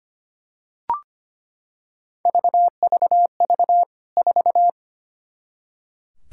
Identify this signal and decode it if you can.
Morse code, heard as a keyed tone, about 25 wpm, 700 Hz. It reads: VVV 4
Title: K1USN SST QSO Practice 25wpm